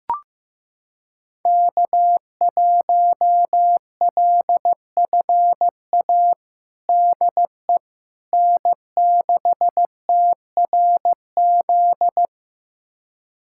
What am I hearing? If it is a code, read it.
Morse code, heard as a keyed tone, about 15 wpm, 700 Hz. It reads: K1LFA DE N6TRZ
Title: USA Callsign DE Callsign - 5 Characters - 15wpm